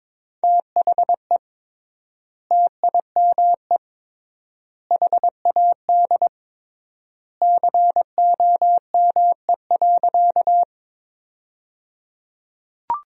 Morse code, heard as a keyed tone, about 22 wpm, 700 Hz. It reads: THE TIME HAD COME.
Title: 3x Word-Spacing - Sentences from Top 100 Words - 22wpm